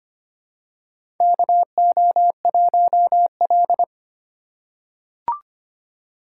Morse code, heard as a keyed tone, about 25 wpm, 700 Hz. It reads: KO1L
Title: USA Call Signs - 4 Characters - 2x Character Spacing - 25wpm